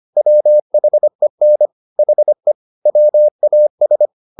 Morse code, heard as a keyed tone, about 25 wpm, 600 Hz. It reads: WHEN HE WAS
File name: the old man and the sea part 1 25wpm